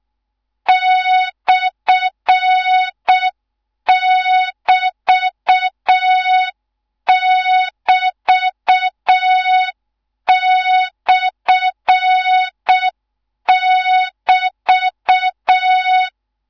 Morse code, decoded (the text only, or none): /<BT><BT>/<BT>